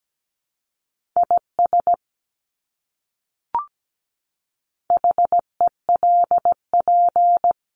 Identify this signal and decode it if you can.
Morse code, heard as a keyed tone, about 17 wpm, 700 Hz. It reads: IS HELP
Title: Top 200 Words Repeated 17wpm